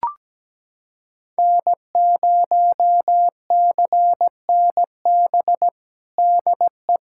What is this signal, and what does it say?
Morse code, heard as a keyed tone, about 17 wpm, 700 Hz. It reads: N0CNB DE
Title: USA Callsign DE Callsign - 5 Characters - 17wpm